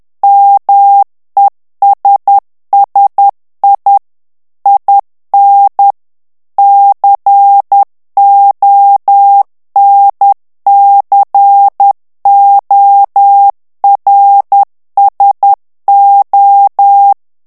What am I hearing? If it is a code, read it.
Morse code, heard as a keyed tone, about 10 wpm, 800 Hz. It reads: MESSI IN CONCORSO
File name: ITALIEN texte clair 2 vitesse 10 mots minute